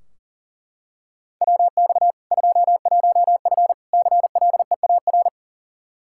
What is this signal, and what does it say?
Morse code, heard as a keyed tone, about 40 wpm, 700 Hz. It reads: WX 21F CLEAR
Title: Practice QSO - Exchange 2 - Info - 40wpm